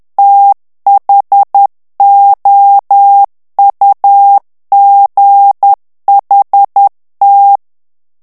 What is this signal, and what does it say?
Morse code, heard as a keyed tone, about 10 wpm, 800 Hz. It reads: THOUGHT
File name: morse mp3 / ANGLAIS texte clair 2 vitesse 10 mots minute